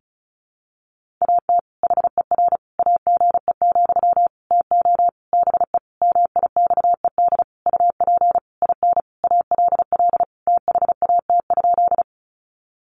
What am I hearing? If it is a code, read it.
Morse code, heard as a keyed tone, about 34 wpm, 700 Hz. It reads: AT HER AGE, TO BE MIXED UP IN ALL THAT?
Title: Sentences from Top 1500 Words 35wpm